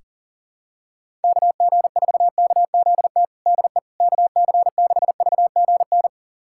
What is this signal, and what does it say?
Morse code, heard as a keyed tone, about 40 wpm, 700 Hz. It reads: KG4KZT DE KC6VGN
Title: USA Callsign DE Callsign 40wpm